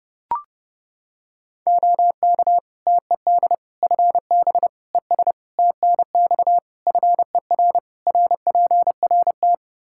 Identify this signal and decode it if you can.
Morse code, heard as a keyed tone, about 32 wpm, 700 Hz. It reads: OK TED FB ES TNX FER RPRT
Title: Practice QSO - Exchange 2 - Info - 30wpm